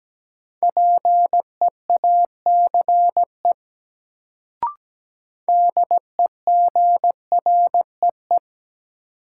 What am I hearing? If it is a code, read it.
Morse code, heard as a keyed tone, about 17 wpm, 700 Hz. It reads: PEACE DEGREE
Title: Sentences from Top 1500 Words - New Word Drill - 17wpm